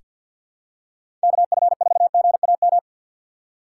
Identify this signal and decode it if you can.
Morse code, heard as a keyed tone, about 46 wpm, 700 Hz. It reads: KF4GAM
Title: USA Call Signs - 4x Character Spacing - 50wpm